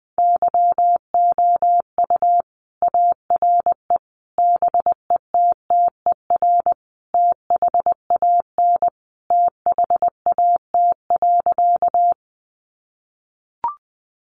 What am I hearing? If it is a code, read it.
Morse code, heard as a keyed tone, about 20 wpm, 700 Hz. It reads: YOU ARE BETTER THAN THAT.
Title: Sentences from Top 400 Words 20wpm